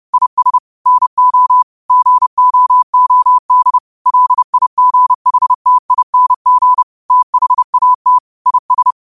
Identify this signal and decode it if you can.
Morse code, heard as a keyed tone, about 30 wpm, 1000 Hz. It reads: IS NO GOOD LIGHTING THAT IS
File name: qotdfiles / 30wpm